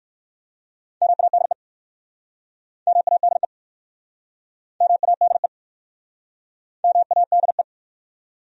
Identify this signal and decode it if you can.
Morse code, heard as a keyed tone, about 65 wpm, 700 Hz. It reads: MADE MADE MADE MADE